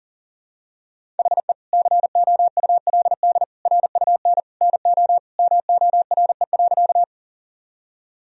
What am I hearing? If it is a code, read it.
Morse code, heard as a keyed tone, about 40 wpm, 700 Hz. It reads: HE COULD RUN NO MORE.
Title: Sentences from Top 1000 Words - Review Part 1 - 40wpm